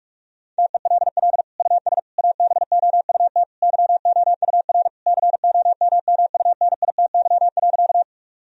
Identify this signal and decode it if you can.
Morse code, heard as a keyed tone, about 46 wpm, 700 Hz. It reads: TELL US ABOUT YOUR COMMUNITY.
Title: Sentences from Top 1000 Words - Review Part 4 - 45wpm